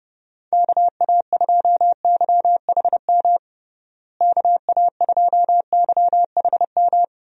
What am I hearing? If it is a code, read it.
Morse code, heard as a keyed tone, about 30 wpm, 700 Hz. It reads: KA2YHM KA2YHM
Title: USA Call Signs - 6 Characters - Repeated 3x - 30wpm